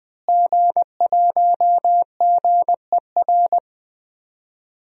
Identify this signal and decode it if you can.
Morse code, heard as a keyed tone, about 20 wpm, 700 Hz. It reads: G1GER
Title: DXCC-100 Call Signs - 5 Characters - Repeated 3x - 20wpm